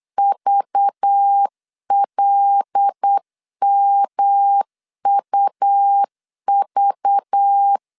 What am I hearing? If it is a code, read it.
Morse code, heard as a keyed tone, about 8 wpm, 800 Hz. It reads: VLMUV
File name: 8wpm